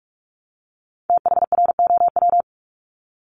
Morse code, heard as a keyed tone, about 46 wpm, 700 Hz. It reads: THROW
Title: Multi-letter Phonemes Repeated 3x 45wpm